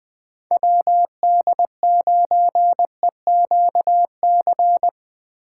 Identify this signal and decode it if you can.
Morse code, heard as a keyed tone, about 21 wpm, 700 Hz. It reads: WD9EQC
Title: USA Call Signs - 6 Characters - 2x Character Spacing - 20wpm